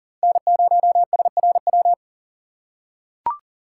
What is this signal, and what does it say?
Morse code, heard as a keyed tone, about 40 wpm, 700 Hz. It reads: N0SRW
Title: USA Call Signs - 5 Characters - 4x Character Spacing - 40wpm